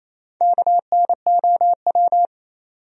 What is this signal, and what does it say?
Morse code, heard as a keyed tone, about 35 wpm, 700 Hz. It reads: KNOW
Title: Speed-Racing Top 200 Words 28wpm